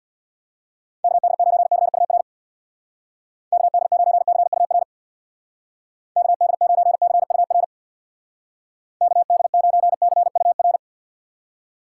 Morse code, heard as a keyed tone, about 59 wpm, 700 Hz. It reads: KD9CUR KD9CUR KD9CUR KD9CUR
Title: Speed Racing - USA 2x3 Call Signs 50wpm